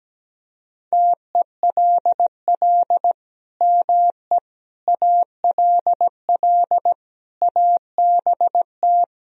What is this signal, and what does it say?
Morse code, heard as a keyed tone, about 17 wpm, 700 Hz. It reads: TELL ME ALL ABT
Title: Sentences from Top 200 Words and Abbreviations 17wpm